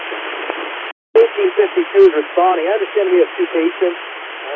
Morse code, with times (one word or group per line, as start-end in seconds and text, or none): none